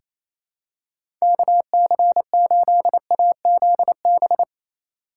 Morse code, heard as a keyed tone, about 28 wpm, 700 Hz. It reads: KC8AZB
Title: USA Call Signs - 8x Character Spacing - 28wpm